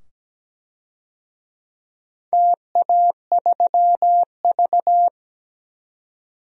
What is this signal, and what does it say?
Morse code, heard as a keyed tone, about 17 wpm, 700 Hz. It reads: TA3V
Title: DXCC-100 Call Signs - 4 Characters - 17wpm